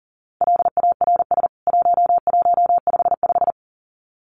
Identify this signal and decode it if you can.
Morse code, heard as a keyed tone, about 40 wpm, 700 Hz. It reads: LARS 1155
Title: CWT Exchange 40wpm